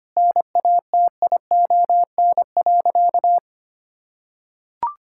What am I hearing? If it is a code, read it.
Morse code, heard as a keyed tone, about 25 wpm, 700 Hz. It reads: NATION.
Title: Sentences from Top 1000 Words - Review Part 6 - 25wpm